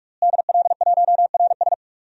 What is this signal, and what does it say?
Morse code, heard as a keyed tone, about 46 wpm, 700 Hz. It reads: DL1RS
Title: DXCC-100 Call Signs - 5 Characters - Repeated 3x - 45wpm